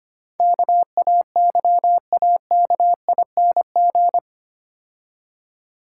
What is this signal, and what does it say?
Morse code, heard as a keyed tone, about 28 wpm, 700 Hz. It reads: KAYAKING
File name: practice / 25wpm / Difficult Words - Speed-Racing with Hints - 25wpm